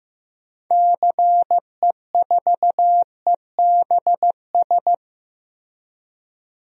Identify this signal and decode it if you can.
Morse code, heard as a keyed tone, about 15 wpm, 700 Hz. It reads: CE4EBS
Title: DXCC-100 Call Signs - 6 Characters - Repeated 2x - 15wpm